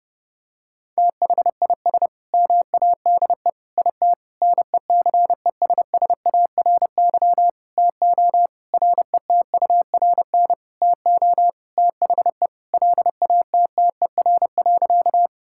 Morse code, heard as a keyed tone, about 30 wpm, 700 Hz. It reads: THIS MADE IT NECESSARY TO RETURN TO THE LATTER.
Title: Sentences from Top 1600 Words 30wpm